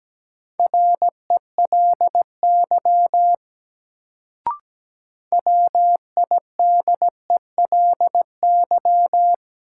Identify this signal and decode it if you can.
Morse code, heard as a keyed tone, about 17 wpm, 700 Hz. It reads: RELY WIDELY
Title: Sentences from Top 1200 Words - New Word Drill Alt - 17wpm